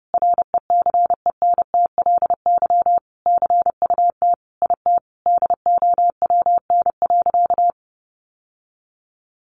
Morse code, heard as a keyed tone, about 30 wpm, 700 Hz. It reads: RECENTLY CUT IT DOWN.
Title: Sentences from Top 1200 Words 30wpm